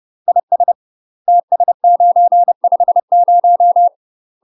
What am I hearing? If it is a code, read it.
Morse code, heard as a keyed tone, about 30 wpm, 700 Hz. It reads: IS TS950
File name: UK MASTER QSOs TRAINING/30WPM